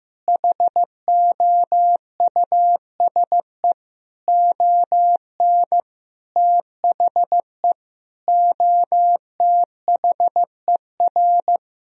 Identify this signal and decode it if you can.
Morse code, heard as a keyed tone, about 15 wpm, 700 Hz. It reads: HOUSE ON THE OTHER
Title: Sentences from Top 1000 Words - Review Part 1 - 15wpm